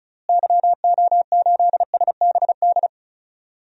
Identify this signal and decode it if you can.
Morse code, heard as a keyed tone, about 36 wpm, 700 Hz. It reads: YO8SBD
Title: DXCC-100 Call Signs - 6 Characters - Repeated 3x - 35wpm